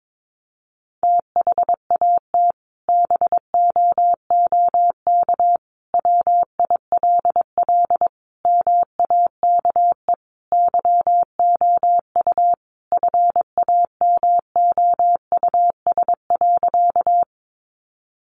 Morse code, heard as a keyed tone, about 22 wpm, 700 Hz. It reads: THAT BOOK WILL MAKE YOU FAMOUS.
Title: Sentences from Top 1500 Words 22wpm